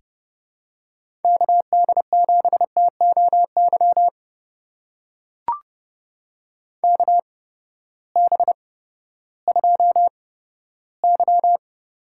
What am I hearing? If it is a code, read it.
Morse code, heard as a keyed tone, about 30 wpm, spaced as for 15 wpm, 700 Hz. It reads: KD7TOY KB2Y
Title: USA Call Signs - 6 Characters - 8x Character Spacing - 30wpm